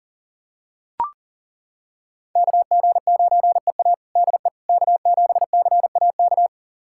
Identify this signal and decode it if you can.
Morse code, heard as a keyed tone, about 37 wpm, 700 Hz. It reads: KG9EA DE K7CAK
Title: USA Callsign DE Callsign - 5 Characters - 40wpm